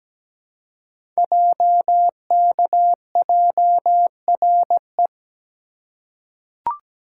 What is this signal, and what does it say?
Morse code, heard as a keyed tone, about 17 wpm, 700 Hz. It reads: JKJRE
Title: Code Groups - 5 Letters - 17wpm